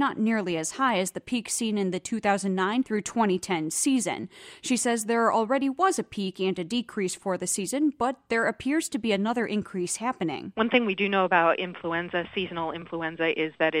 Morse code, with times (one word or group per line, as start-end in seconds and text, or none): none